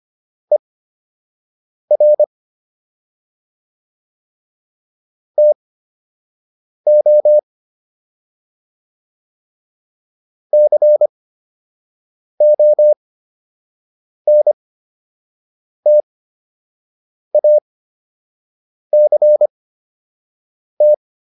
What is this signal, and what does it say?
Morse code, heard as a keyed tone, about 25 wpm, spaced as for 6 wpm, 600 Hz. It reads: ER TO CONTACT